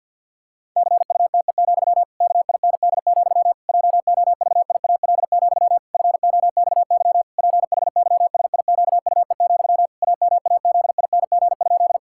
Morse code, heard as a keyed tone, about 50 wpm, 700 Hz. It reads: CUTE, KIND, JOVIAL, FOXY PHYSIQUE, AMAZING?
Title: Pangram Challenge v2 - 50wpm